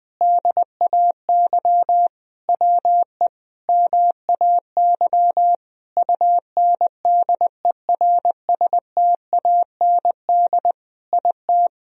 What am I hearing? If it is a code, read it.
Morse code, heard as a keyed tone, about 20 wpm, 700 Hz. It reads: DAY WE MAY UNDERSTAND IT
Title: Sentences from Top 1000 Words - Review Part 4 - 20wpm